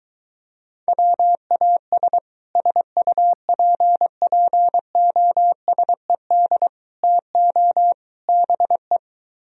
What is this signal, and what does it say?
Morse code, heard as a keyed tone, about 24 wpm, 700 Hz. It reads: WAS SUPPOSED TO BE